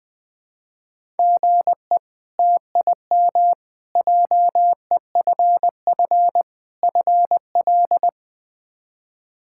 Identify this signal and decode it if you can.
Morse code, heard as a keyed tone, about 20 wpm, 700 Hz. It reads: GE TIM JEFF FL
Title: K1USN SST QSO Practice - 20wpm